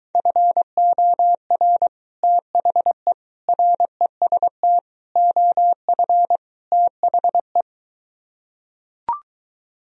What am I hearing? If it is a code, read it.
Morse code, heard as a keyed tone, about 23 wpm, 700 Hz. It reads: FOR THE REST OF THE